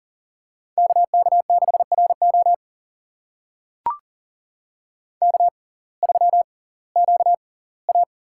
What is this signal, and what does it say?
Morse code, heard as a keyed tone, about 40 wpm, spaced as for 22 wpm, 700 Hz. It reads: KK6RO K3QA
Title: USA Call Signs - 6x Character Spacing - 40wpm